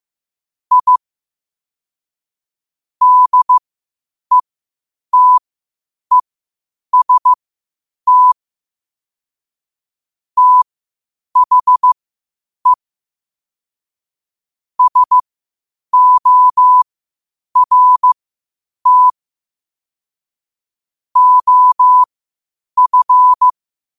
Morse code, heard as a keyed tone, about 15 wpm, spaced as for 8 wpm, 1000 Hz. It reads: I DETEST THE SORT OF
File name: qotdfiles / 8wpm